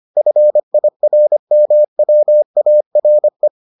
Morse code, heard as a keyed tone, about 25 wpm, 600 Hz. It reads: FIRMWARE